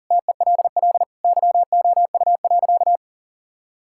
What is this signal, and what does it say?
Morse code, heard as a keyed tone, about 40 wpm, 700 Hz. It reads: TELL YOU.
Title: Sentences from Top 1600 Words 40wpm